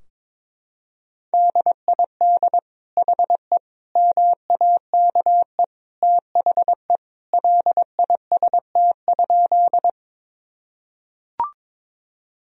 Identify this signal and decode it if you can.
Morse code, heard as a keyed tone, about 22 wpm, 700 Hz. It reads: DID HE MAKE THE LIST?
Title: Sentences from Top 1000 Words - Review Part 2 - 22wpm